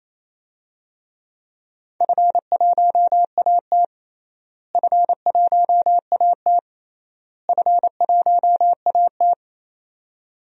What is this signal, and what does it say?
Morse code, heard as a keyed tone, about 28 wpm, 700 Hz. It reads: F1AT F1AT F1AT
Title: DXCC-100 Call Signs - 4 Characters - Repeated 3x - 28wpm